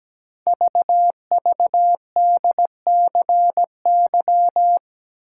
Morse code, heard as a keyed tone, about 17 wpm, 700 Hz. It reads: VVDCY
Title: Code Groups - 5 Letters - 17wpm